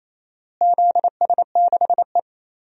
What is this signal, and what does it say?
Morse code, heard as a keyed tone, about 28 wpm, 700 Hz. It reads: ZS6E
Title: DXCC-100 Call Signs - 4 Characters - Repeated 2x - 28wpm